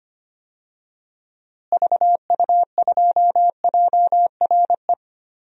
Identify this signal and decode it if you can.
Morse code, heard as a keyed tone, about 25 wpm, 700 Hz. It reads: VU2JRE
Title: DXCC-100 Call Signs - 6 Characters - Repeated 3x - 25wpm